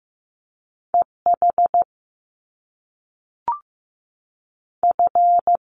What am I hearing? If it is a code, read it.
Morse code, heard as a keyed tone, about 15 wpm, 700 Hz. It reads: E H F
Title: Top Two Letter Combinations in Words 15wpm